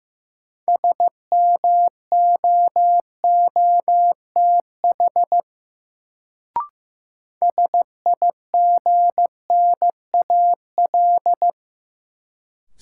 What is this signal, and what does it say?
Morse code, heard as a keyed tone, about 15 wpm, 700 Hz. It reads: SMOOTH SIGNAL
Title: Sentences from Top 1400 Words - New Word Drill - 15wpm